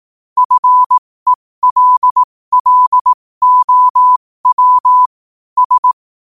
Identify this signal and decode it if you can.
Morse code, heard as a keyed tone, about 18 wpm, spaced as for 15 wpm, 1000 Hz. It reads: FELLOW S
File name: qotdfiles / 15wpm